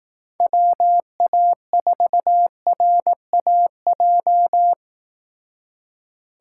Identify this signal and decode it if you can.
Morse code, heard as a keyed tone, about 18 wpm, 700 Hz. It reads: WA4RAJ
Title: Speed Racing - USA Call Signs 15wpm